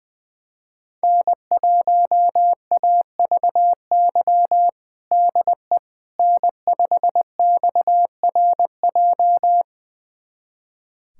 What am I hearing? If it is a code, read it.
Morse code, heard as a keyed tone, about 20 wpm, 700 Hz. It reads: N1AVY DE N5XRJ
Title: USA Callsign DE Callsign - 5 Characters - 20wpm